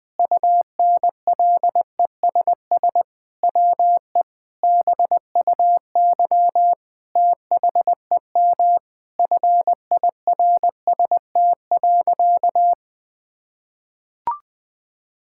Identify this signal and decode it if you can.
Morse code, heard as a keyed tone, about 20 wpm, 700 Hz. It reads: UNLESS WE BUY THEM FIRST.
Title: Sentences from Top 1000 Words - Review Part 6 - 20wpm